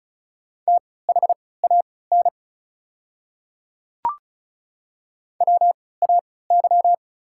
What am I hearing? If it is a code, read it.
Morse code, heard as a keyed tone, about 36 wpm, spaced as for 20 wpm, 690 Hz. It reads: THAN WAY
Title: 3x Character Spacing - Top 100 Words - 35wpm